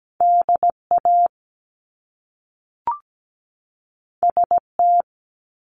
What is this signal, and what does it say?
Morse code, heard as a keyed tone, about 17 wpm, 700 Hz. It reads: DA ST
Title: Top Two Letter Combinations in Words 17wpm